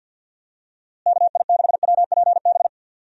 Morse code, heard as a keyed tone, about 46 wpm, 700 Hz. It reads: KI6WPB
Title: USA Call Signs - 6 Characters - Repeated 3x - 50wpm